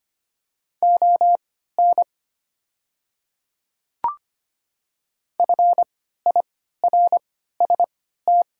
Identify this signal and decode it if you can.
Morse code, heard as a keyed tone, about 25 wpm, spaced as for 14 wpm, 690 Hz. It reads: ON FIRST
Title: 3x Character Spacing - Top 200 Words - 25wpm